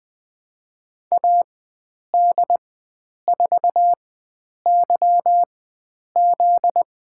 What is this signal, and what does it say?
Morse code, heard as a keyed tone, about 21 wpm, spaced as for 9 wpm, 700 Hz. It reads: AD4YZ
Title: USA Call Signs - 5 Characters - 4x Character Spacing - 20wpm